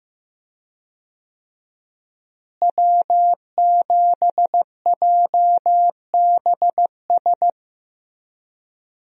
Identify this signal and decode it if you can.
Morse code, heard as a keyed tone, about 15 wpm, 700 Hz. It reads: W7JBS
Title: USA Call Signs - 5 Characters - Repeated 2x - 15wpm